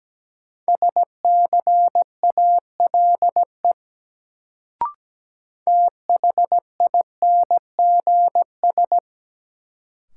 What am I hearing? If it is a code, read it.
Morse code, heard as a keyed tone, about 17 wpm, 700 Hz. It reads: SCALE THINGS
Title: Sentences from Top 1000 Words - Review Part 4 - New Word Drill - 17wpm